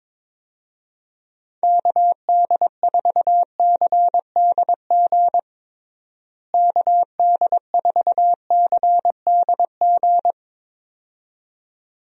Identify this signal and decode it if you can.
Morse code, heard as a keyed tone, about 22 wpm, 700 Hz. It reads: KD4CDG KD4CDG
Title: USA Call Signs - 6 Characters - Repeated 2x - 22wpm